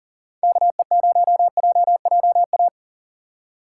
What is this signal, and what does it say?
Morse code, heard as a keyed tone, about 40 wpm, 700 Hz. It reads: KE0JJA
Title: USA Call Signs - 6 Characters - 3x Character Spacing - 40wpm